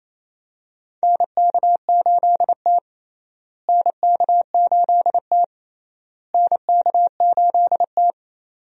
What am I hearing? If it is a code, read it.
Morse code, heard as a keyed tone, about 28 wpm, 700 Hz. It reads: NK8T NK8T NK8T
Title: USA Call Signs - 4 Characters - Repeated 3x - 28wpm